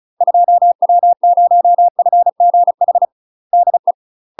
Morse code, heard as a keyed tone, about 34 wpm, 700 Hz. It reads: 2W0FGH DE
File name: UK MASTER QSOs TRAINING / 35WPM / UK MASTER QSOs 0024